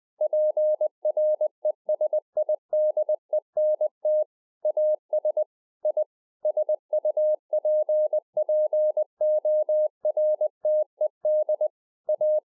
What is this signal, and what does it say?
Morse code, heard as a keyed tone, about 20 wpm, 610 Hz. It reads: PRESIDENT AS I SUPPORTED A